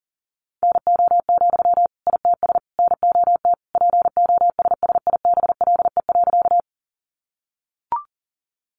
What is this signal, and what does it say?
Morse code, heard as a keyed tone, about 40 wpm, 700 Hz. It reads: NO, ITS NOT POSSIBLE.
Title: Sentences from Top 600 Words 40wpm